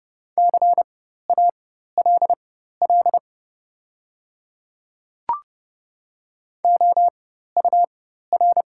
Morse code, heard as a keyed tone, about 30 wpm, spaced as for 14 wpm, 700 Hz. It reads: CALL OUR
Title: 4x Character Spacing - Top 200 Words - 30wpm